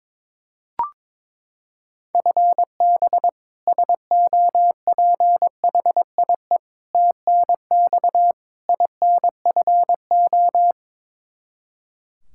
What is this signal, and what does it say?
Morse code, heard as a keyed tone, about 22 wpm, 700 Hz. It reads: FB SOPHIE TNX INFO
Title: K1USN SST QSO Practice 22wpm